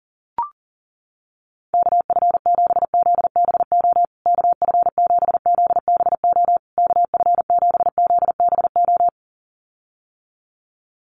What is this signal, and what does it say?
Morse code, heard as a keyed tone, about 40 wpm, 700 Hz. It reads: KF7ZBO KF7ZBO KF7ZBO
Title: USA Call Signs Repeated 3x 40wpm